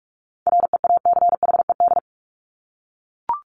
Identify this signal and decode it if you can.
Morse code, heard as a keyed tone, about 44 wpm, 700 Hz. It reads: REACHED
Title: ED Suffix - Top 100 Words with Base Words 45wpm